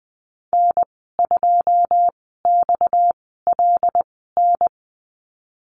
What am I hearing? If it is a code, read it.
Morse code, heard as a keyed tone, about 20 wpm, spaced as for 15 wpm, 700 Hz. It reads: N2XLN
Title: 231 Lesson 37g - Letter x - Call Signs 20wpm